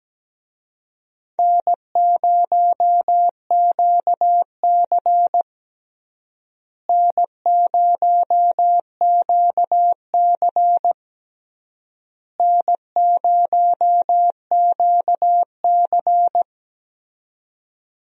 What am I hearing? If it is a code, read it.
Morse code, heard as a keyed tone, about 17 wpm, 700 Hz. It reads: N0QC N0QC N0QC
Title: USA Call Signs - 4 Characters - Repeated 3x - 17wpm